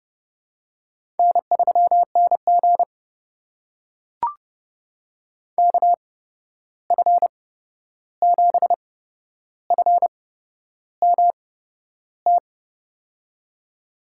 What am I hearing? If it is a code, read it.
Morse code, heard as a keyed tone, about 32 wpm, spaced as for 11 wpm, 700 Hz. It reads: N3NG KF7FMT
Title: USA Call Signs - 8x Character Spacing - 30wpm